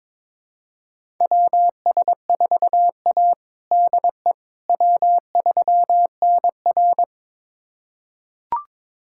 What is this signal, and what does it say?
Morse code, heard as a keyed tone, about 22 wpm, 700 Hz. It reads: WS4A DE W3NR
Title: USA Callsign DE Callsign - 4 Characters - 22wpm